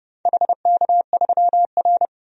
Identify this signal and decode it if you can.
Morse code, heard as a keyed tone, about 30 wpm, 700 Hz. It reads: HK3R
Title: DXCC-100 Call Signs - 4 Characters - 30wpm